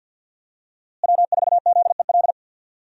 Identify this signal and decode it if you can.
Morse code, heard as a keyed tone, about 52 wpm, 700 Hz. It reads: W4ZEL
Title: USA Call Signs - 5 Characters - Repeated 2x - 50wpm